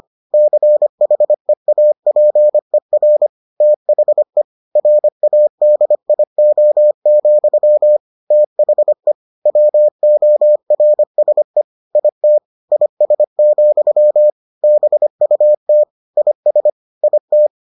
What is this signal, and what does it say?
Morse code, heard as a keyed tone, about 25 wpm, 600 Hz. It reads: CHEAPER THE RADIO, THE WORSE IT IS, BUT IS IT